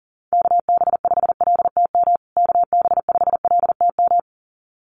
Morse code, heard as a keyed tone, about 37 wpm, 700 Hz. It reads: KB5LTM KB5LTM
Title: USA Call Signs Repeated 2x 40wpm